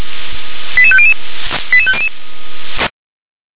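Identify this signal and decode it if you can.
Five-tone selective call (ZVEI), heard as quick runs of five tones, tones of about 70 ms. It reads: 88500 88500
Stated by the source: ZVEI five-tone selective call received by SDR.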